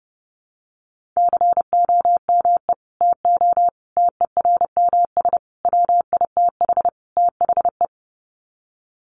Morse code, heard as a keyed tone, about 30 wpm, 700 Hz. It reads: COME TO TERMS WITH THE
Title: Sets of 5 Words Encore2 30wpm